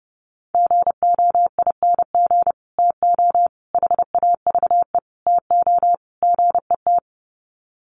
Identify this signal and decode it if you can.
Morse code, heard as a keyed tone, about 30 wpm, 700 Hz. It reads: GOING TO HAVE TO GET
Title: Sets of 5 Words Encore2 30wpm